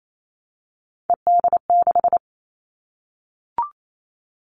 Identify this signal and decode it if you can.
Morse code, heard as a keyed tone, about 28 wpm, 700 Hz. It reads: ED6
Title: Callsign Prefixes for Top 50 DXCC - 28wpm